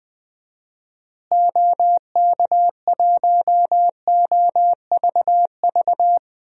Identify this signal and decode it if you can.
Morse code, heard as a keyed tone, about 20 wpm, 700 Hz. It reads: OK1OVV
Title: DXCC-100 Call Signs - 6 Characters - Repeated 2x - 20wpm